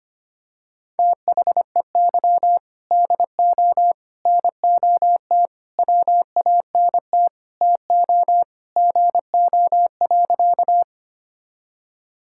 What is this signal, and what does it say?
Morse code, heard as a keyed tone, about 25 wpm, 700 Hz. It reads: THEY DO NOT WANT TO GO.
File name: practice / 25wpm / Sentences with 3-Word Phrases - 25wpm